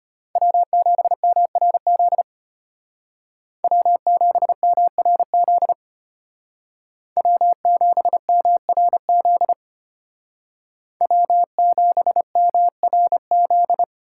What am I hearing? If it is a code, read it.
Morse code, heard as a keyed tone, about 31 wpm, 700 Hz. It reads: W7MRZ W7MRZ W7MRZ W7MRZ
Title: Speed Racing - USA 1x3 Call Signs 25wpm